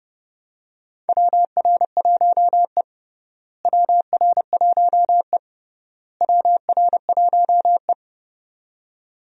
Morse code, heard as a keyed tone, about 30 wpm, 700 Hz. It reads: WR1E WR1E WR1E
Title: USA Call Signs - 4 Characters - Repeated 3x - 30wpm